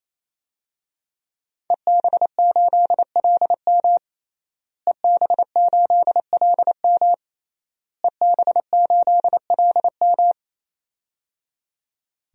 Morse code, heard as a keyed tone, about 28 wpm, 700 Hz. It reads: EB8LM EB8LM EB8LM
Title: DXCC-100 Call Signs - 5 Characters - Repeated 3x - 28wpm